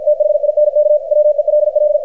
Morse code, heard as a keyed tone, about 53 wpm, 600 Hz, with radio noise. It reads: THINK WELL
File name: archived / CRSnet